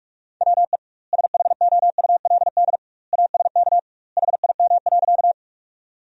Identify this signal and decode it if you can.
Morse code, heard as a keyed tone, about 48 wpm, 700 Hz. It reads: WE SHOULD ASK HIM.
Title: 2x Word-Spacing - Sentences from Top 300 Words - 45wpm